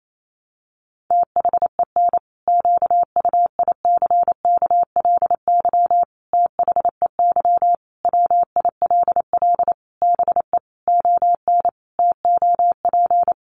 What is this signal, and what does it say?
Morse code, heard as a keyed tone, about 28 wpm, 700 Hz. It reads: THEN QUICKLY THEY WILL BE ON TOP
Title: Sentences from Top 700 Words 28wpm